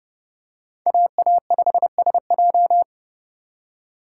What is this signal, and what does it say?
Morse code, heard as a keyed tone, about 29 wpm, 700 Hz. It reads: AA5SJ